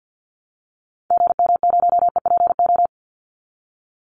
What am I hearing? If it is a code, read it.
Morse code, heard as a keyed tone, about 50 wpm, 700 Hz. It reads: GM0EPO
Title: Call Signs - 50wpm